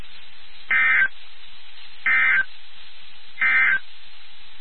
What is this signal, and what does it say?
Emergency Alert System (SAME) tones, heard as harsh buzzing bursts. It is heard from 0.6 to 3.9 s.
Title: NOAA Weather Radio WXK79